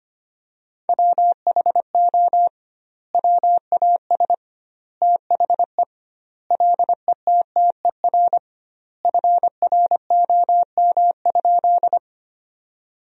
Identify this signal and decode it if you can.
Morse code, heard as a keyed tone, about 25 wpm, 700 Hz. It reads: WHO WAS THE LETTER FROM?
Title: 2x Word-Spacing - Sentences from Top 300 Words - 25wpm